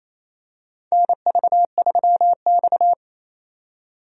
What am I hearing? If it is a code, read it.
Morse code, heard as a keyed tone, about 28 wpm, 700 Hz. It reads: NV3X
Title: USA Call Signs - 4 Characters - 28wpm